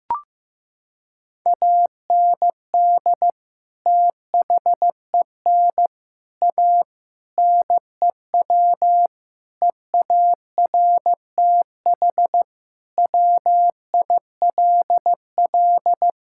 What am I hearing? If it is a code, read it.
Morse code, heard as a keyed tone, about 16 wpm, 700 Hz. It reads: AND THEN A NEW EARTH WILL
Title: Sentences from Top 1000 Words - Review Part 6 - 15wpm